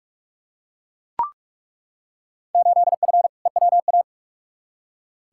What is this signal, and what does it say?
Morse code, heard as a keyed tone, about 42 wpm, 700 Hz. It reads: 8F EWA
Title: Field Day Exchanges Encore - 45wpm